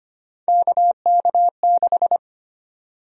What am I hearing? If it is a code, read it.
Morse code, heard as a keyed tone, about 25 wpm, 700 Hz. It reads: KK6
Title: USA Call Sign Prefixes - Repeated 2x - 25wpm